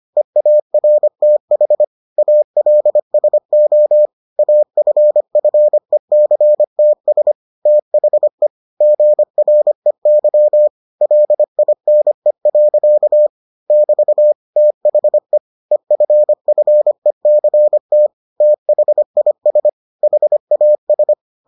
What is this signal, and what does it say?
Morse code, heard as a keyed tone, about 25 wpm, 600 Hz. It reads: EARTH ALSO AFFECTS THE GREY LINE. <BT> THE EFFECT THIS HAS